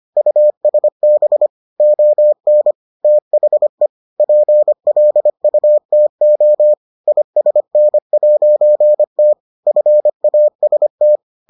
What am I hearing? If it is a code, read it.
Morse code, heard as a keyed tone, about 25 wpm, 600 Hz. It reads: USB ON THE PLUTO ISN'T FAST